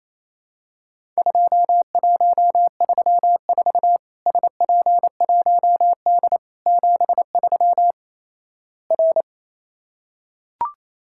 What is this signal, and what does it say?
Morse code, heard as a keyed tone, about 27 wpm, 700 Hz. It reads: 2134 SP1D 73 R
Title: Worked All Europe DX Contest - QTC Exchange - 28wpm